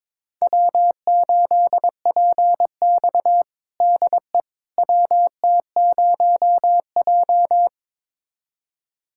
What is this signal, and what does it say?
Morse code, heard as a keyed tone, about 22 wpm, 700 Hz. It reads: W8PX DE WT0J
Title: USA Callsign DE Callsign - 4 Characters - 22wpm